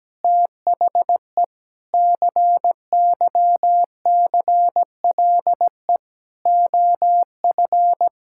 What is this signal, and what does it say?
Morse code, heard as a keyed tone, about 17 wpm, 700 Hz. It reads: THE CYCLE OF
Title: Sentences from Top 1000 Words - Review Part 6 - 17wpm